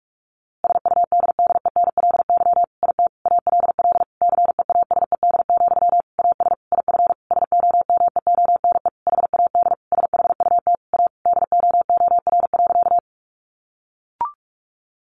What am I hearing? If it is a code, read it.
Morse code, heard as a keyed tone, about 44 wpm, 700 Hz. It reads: SUDDENLY IT ALL CEASED, AS IF SOMEONE HAD SHUT A DOOR.
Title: A Sound Of Thunder 45wpm